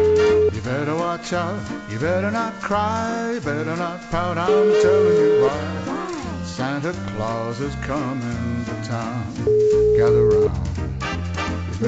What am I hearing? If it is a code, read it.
Ringback tone (Europe, 425 Hz), with speech.